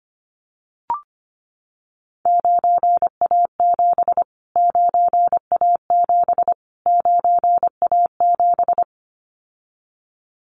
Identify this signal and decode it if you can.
Morse code, heard as a keyed tone, about 25 wpm, 700 Hz. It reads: 9A7 9A7 9A7
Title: Callsign Prefixes for Top 25 DXCC - Repeated 3x - 25wpm